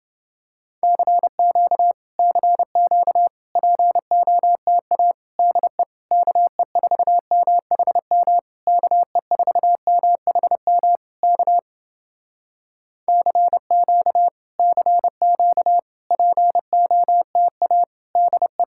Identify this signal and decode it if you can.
Morse code, heard as a keyed tone, about 30 wpm, 700 Hz. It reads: CQ CQ POTA DE KE4MHM KE4MHM K CQ CQ POTA DE